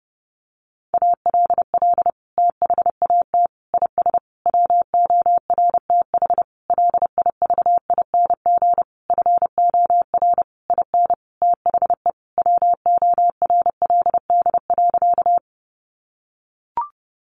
Morse code, heard as a keyed tone, about 30 wpm, 700 Hz. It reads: ALL THAT IS WORTH LIVING FOR IN THE WORLD.
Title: Sentences from Top 1100 Words 30wpm